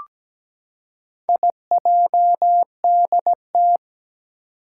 Morse code, heard as a keyed tone, about 17 wpm, 700 Hz. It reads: IJDT
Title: Four Challenging Letter-Number Combinations 17wpm